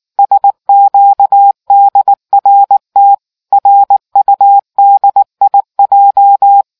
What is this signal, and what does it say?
Morse code, heard as a keyed tone, about 19 wpm, 800 Hz. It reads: SQDRT RUDIJ